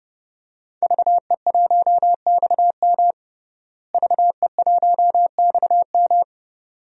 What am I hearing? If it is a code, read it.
Morse code, heard as a keyed tone, about 31 wpm, 700 Hz. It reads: VE1XM VE1XM
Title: DXCC-100 Call Signs - 5 Characters - Repeated 2x - 30wpm